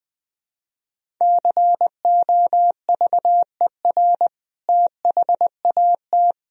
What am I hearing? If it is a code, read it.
Morse code, heard as a keyed tone, about 20 wpm, 700 Hz. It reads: COVER THAT
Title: Sentences from Top 1000 Words - Review Part 1 - 20wpm